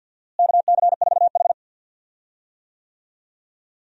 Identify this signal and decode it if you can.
Morse code, heard as a keyed tone, about 46 wpm, 700 Hz. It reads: KC4H
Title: USA Call Signs - 4 Characters - Repeated 2x - 50wpm